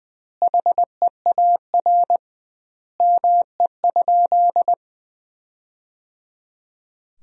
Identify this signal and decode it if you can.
Morse code, heard as a keyed tone, about 20 wpm, 700 Hz. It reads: HEAR ME?
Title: 2x Word-Spacing - Sentences from Top 300 Words - 20wpm